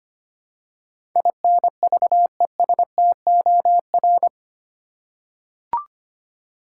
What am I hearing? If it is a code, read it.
Morse code, heard as a keyed tone, about 25 wpm, 700 Hz. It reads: INVESTOR
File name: 8 Letter Words - First 3 Letters Given - 25wpm